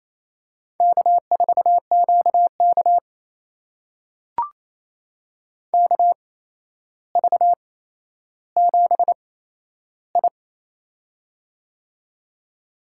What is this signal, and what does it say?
Morse code, heard as a keyed tone, about 28 wpm, spaced as for 12 wpm, 700 Hz. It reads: K4QK KV7I
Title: USA Call Signs - 4 Characters - 8x Character Spacing - 28wpm